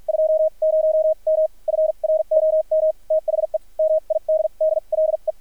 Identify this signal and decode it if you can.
Morse code, heard as a keyed tone, about 53 wpm, spaced as for 41 wpm, 630 Hz. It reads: 20M FROM THE MIDDLE
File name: archived / CRSnet